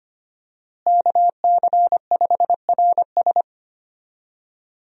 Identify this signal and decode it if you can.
Morse code, heard as a keyed tone, about 25 wpm, 700 Hz. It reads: KC5RS
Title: USA Call Signs - 5 Characters - Repeated 2x - 25wpm